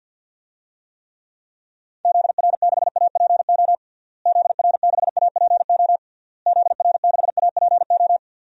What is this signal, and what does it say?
Morse code, heard as a keyed tone, about 51 wpm, 700 Hz. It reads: ZR6APO ZR6APO ZR6APO
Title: DXCC-100 Call Signs - 6 Characters - Repeated 3x - 50wpm